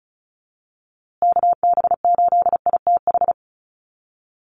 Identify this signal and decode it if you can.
Morse code, heard as a keyed tone, about 34 wpm, 700 Hz. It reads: KB8ITH
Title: USA Call Signs Repeated 2x 35wpm